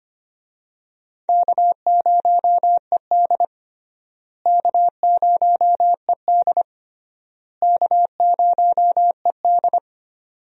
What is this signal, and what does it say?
Morse code, heard as a keyed tone, about 25 wpm, 700 Hz. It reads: K0ED K0ED K0ED
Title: USA Call Signs - 4 Characters - Repeated 3x - 25wpm